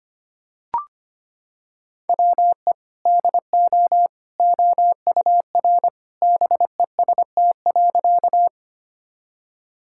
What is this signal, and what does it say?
Morse code, heard as a keyed tone, about 25 wpm, 700 Hz. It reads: WE DO OUR BEST.
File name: practice / 25wpm / Sentences with 2-Word Phrases - 25wpm